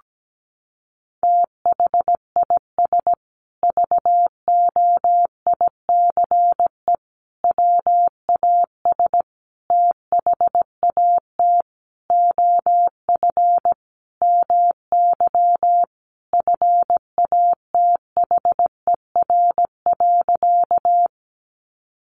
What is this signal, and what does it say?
Morse code, heard as a keyed tone, about 17 wpm, 700 Hz. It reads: THIS VOICE WAS THAT OF MY FATHER.
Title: Sentences from Top 500 Words 17wpm